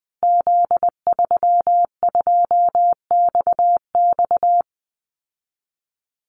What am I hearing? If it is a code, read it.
Morse code, heard as a keyed tone, about 20 wpm, 700 Hz. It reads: Z32XX
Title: Call Signs Repeated 2x - 20wpm